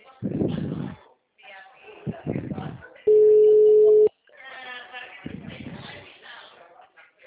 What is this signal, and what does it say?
Ringback tone (Europe, 425 Hz), with speech.
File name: out- -2044- - - .13686